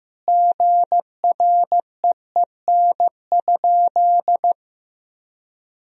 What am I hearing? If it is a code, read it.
Morse code, heard as a keyed tone, about 15 wpm, 700 Hz. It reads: GREEN?
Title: Sentences from Top 1000 Words - Review Part 2 - 15wpm